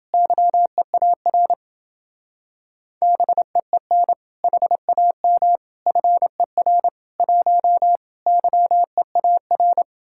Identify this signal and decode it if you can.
Morse code, heard as a keyed tone, about 30 wpm, 700 Hz. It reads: YEAR BEEN HAM FER 1 YEAR